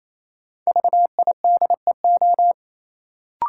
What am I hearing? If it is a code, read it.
Morse code, heard as a keyed tone, about 28 wpm, 700 Hz. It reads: VIDEO
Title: Sentences from Top 1000 Words - Review Part 4 - New Word Drill - 28wpm